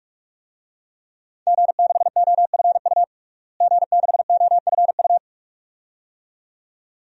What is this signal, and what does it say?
Morse code, heard as a keyed tone, about 42 wpm, 700 Hz. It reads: G6OFU G6OFU
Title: DXCC-100 Call Signs - 5 Characters - Repeated 2x - 45wpm